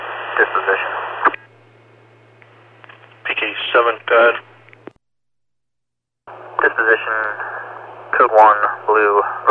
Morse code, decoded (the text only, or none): none